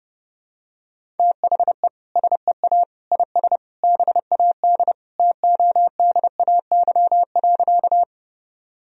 Morse code, heard as a keyed tone, about 30 wpm, 700 Hz. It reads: THE SEA IS BAD TODAY.
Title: Sentences from Top 1000 Words - Review Part 1 - 30wpm